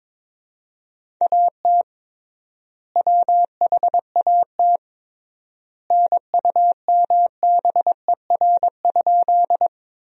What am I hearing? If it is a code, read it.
Morse code, heard as a keyed tone, about 22 wpm, 700 Hz. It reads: AT WHAT NUMBER?
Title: 3x Word-Spacing - Sentences from Top 100 Words - 22wpm